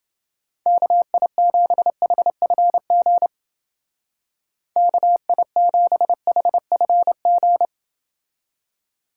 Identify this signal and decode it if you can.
Morse code, heard as a keyed tone, about 29 wpm, 700 Hz. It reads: KI7HFG KI7HFG